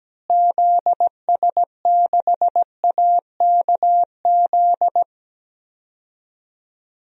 Morse code, heard as a keyed tone, about 17 wpm, 700 Hz. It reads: ZS6AKZ
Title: DXCC-100 Call Signs - 6 Characters - Repeated 3x - 17wpm